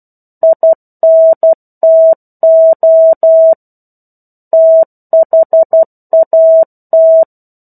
Morse code, heard as a keyed tone, about 12 wpm, 650 Hz. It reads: INTO THAT
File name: 01-Chapter-01 12 WPM